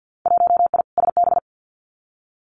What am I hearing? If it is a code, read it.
Morse code, heard as a keyed tone, about 50 wpm, 700 Hz. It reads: 1I SB
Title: Winter Field Day Exchanges Encore 50wpm